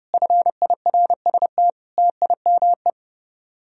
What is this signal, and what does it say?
Morse code, heard as a keyed tone, about 36 wpm, 700 Hz. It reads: FIRST TIME
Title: Speed-Racing - Sets of 5 Words 30wpm